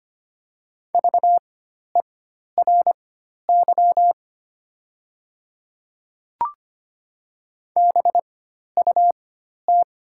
Morse code, heard as a keyed tone, about 25 wpm, spaced as for 12 wpm, 700 Hz. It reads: VERY BUT
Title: 4x Character Spacing - Top 300 Words - 25wpm